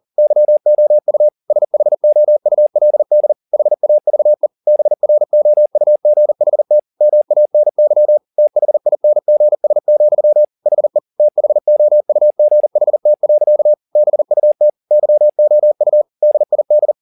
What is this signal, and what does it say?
Morse code, heard as a keyed tone, about 40 wpm, 600 Hz. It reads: YOU SHOULD HAVE BROUGHT MANY THINGS, HE THOUGHT. BUT YOU DID
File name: the old man and the sea 2 40wpm